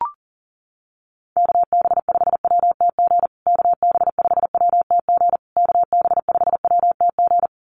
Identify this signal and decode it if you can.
Morse code, heard as a keyed tone, about 40 wpm, 700 Hz. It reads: KB5WTG KB5WTG KB5WTG
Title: USA Call Signs Repeated 3x 40wpm